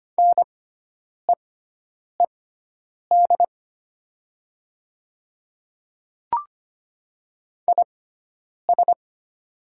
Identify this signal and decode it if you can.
Morse code, heard as a keyed tone, about 25 wpm, spaced as for 9 wpm, 700 Hz. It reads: NEED IS
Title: 6x Character Spacing - Top 300 Words - 25wpm